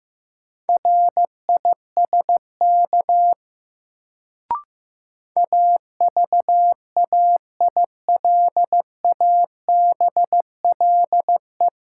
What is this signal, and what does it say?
Morse code, heard as a keyed tone, about 16 wpm, 700 Hz. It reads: RISK AVAILABLE
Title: Sentences from Top 600 Words - New Word Drill - 15wpm